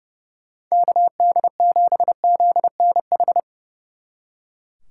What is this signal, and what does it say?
Morse code, heard as a keyed tone, about 38 wpm, 700 Hz. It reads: KD7ZNH